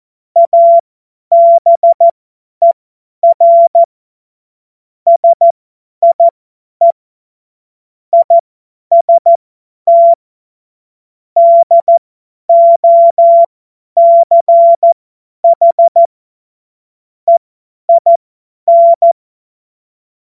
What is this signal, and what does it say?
Morse code, heard as a keyed tone, about 14 wpm, spaced as for 10 wpm, 680 Hz. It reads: ABER SIE IST DOCH EIN
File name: Witz-12-August-2025-50-F70